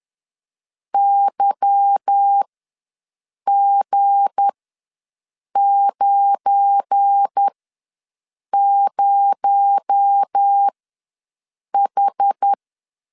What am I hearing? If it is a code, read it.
Morse code, heard as a keyed tone, about 10 wpm, spaced as for 6 wpm, 800 Hz. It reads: YG90H